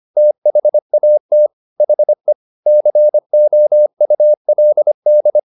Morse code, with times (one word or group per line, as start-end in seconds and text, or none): 0.2-1.5 THAT
1.8-2.3 HE
2.7-5.4 COULD